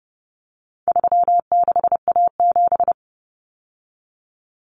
Morse code, heard as a keyed tone, about 30 wpm, 700 Hz. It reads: 36A7
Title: Common Cut Numbers - 4-Digit 30wpm